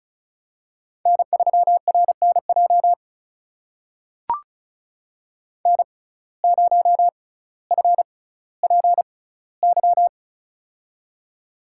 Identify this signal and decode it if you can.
Morse code, heard as a keyed tone, about 36 wpm, spaced as for 18 wpm, 700 Hz. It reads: N3RNJ N0FPY
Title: USA Call Signs - 5 Characters - 6x Character Spacing - 35wpm